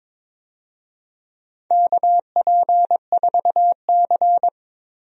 Morse code, heard as a keyed tone, about 22 wpm, 700 Hz. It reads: KP4C
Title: DXCC-100 Call Signs - 4 Characters - Repeated 3x - 22wpm